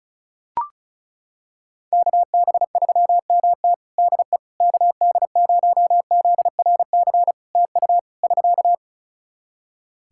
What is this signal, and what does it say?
Morse code, heard as a keyed tone, about 34 wpm, 700 Hz. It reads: KB3MT DE KD0ZRC TU <SK>
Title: Practice QSO - Exchange 4 - Ending - 35wpm